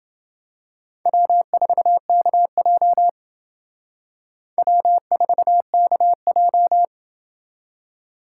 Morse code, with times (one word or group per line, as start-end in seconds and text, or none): 1.1-3.1 W4KJ
4.6-6.9 W4KJ